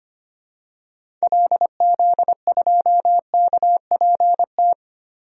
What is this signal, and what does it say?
Morse code, heard as a keyed tone, about 25 wpm, 700 Hz. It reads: LZ2KPT
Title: DXCC-100 Call Signs - 6 Characters - Repeated 3x - 25wpm